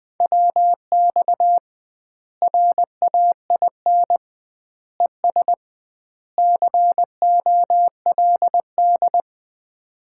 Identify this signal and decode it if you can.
Morse code, heard as a keyed tone, about 20 wpm, 700 Hz. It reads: WX RAIN ES COLD
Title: Common QSO Phrases - 2x Word-Spacing - 20wpm